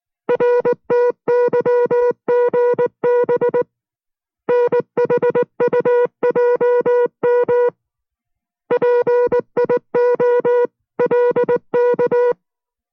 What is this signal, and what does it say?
Morse code, heard as a keyed tone, about 19 wpm, 490 Hz. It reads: RTYGB NHUJM PIOLK